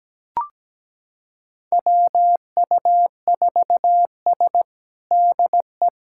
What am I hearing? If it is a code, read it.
Morse code, heard as a keyed tone, about 17 wpm, 700 Hz. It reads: WU4S DE
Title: USA Callsign DE Callsign - 4 Characters - 17wpm